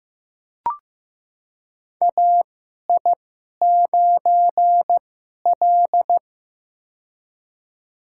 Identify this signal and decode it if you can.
Morse code, heard as a keyed tone, about 15 wpm, spaced as for 11 wpm, 700 Hz. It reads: AI9L
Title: USA Call Signs - 2x Character Spacing - 15wpm